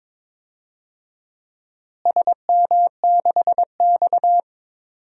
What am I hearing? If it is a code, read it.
Morse code, heard as a keyed tone, about 22 wpm, 700 Hz. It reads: SM6X
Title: DXCC-100 Call Signs - 4 Characters - Repeated 2x - 22wpm